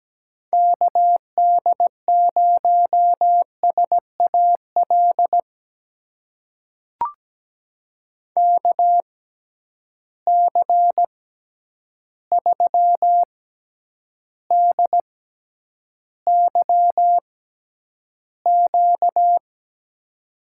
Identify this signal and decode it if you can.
Morse code, heard as a keyed tone, about 17 wpm, spaced as for 9 wpm, 700 Hz. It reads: KD0SAL KC3DYQ
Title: USA Call Signs - 6 Characters - 6x Character Spacing - 17wpm